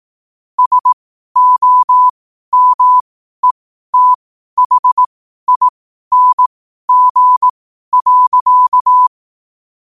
Morse code, heard as a keyed tone, about 18 wpm, spaced as for 12 wpm, 1000 Hz. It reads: SOMETHING.
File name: qotdfiles / 12wpm